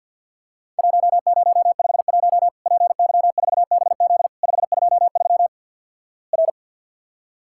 Morse code, heard as a keyed tone, about 54 wpm, 700 Hz. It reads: 1051 PY4BZ 523 R
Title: Worked All Europe DX Contest - QTC Exchange - 50wpm